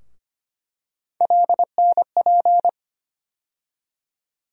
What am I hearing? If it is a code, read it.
Morse code, heard as a keyed tone, about 25 wpm, 700 Hz. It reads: LNP
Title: USA Call Sign Postfixes - 3 Letters 25wpm